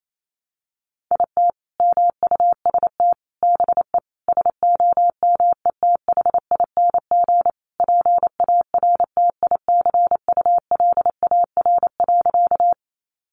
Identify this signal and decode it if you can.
Morse code, heard as a keyed tone, about 28 wpm, 700 Hz. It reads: IT MUST BE SOMETHING PARTICULAR.
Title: Sentences from Top 600 Words 28wpm